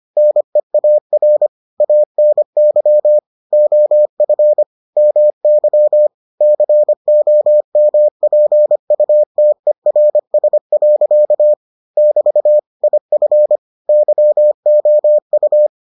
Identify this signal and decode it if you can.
Morse code, heard as a keyed tone, about 25 wpm, 600 Hz. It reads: NEAR ANY OF MY COMPUTERS. <BT> IF YOU